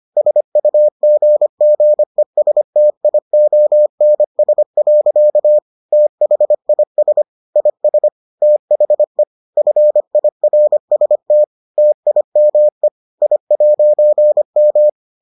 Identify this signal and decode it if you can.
Morse code, heard as a keyed tone, about 25 wpm, 600 Hz. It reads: SUGGESTIONS. THIS IS THE FIRST TIME I'M